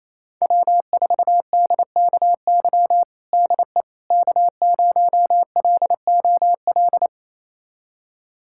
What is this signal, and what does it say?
Morse code, heard as a keyed tone, about 28 wpm, 700 Hz. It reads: W4DKY DE K0LOL
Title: USA Callsign DE Callsign - 5 Characters - 28wpm